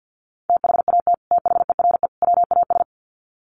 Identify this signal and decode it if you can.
Morse code, heard as a keyed tone, about 50 wpm, 700 Hz. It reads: THAT THERE WAS
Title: Sets of 3 Words 50wpm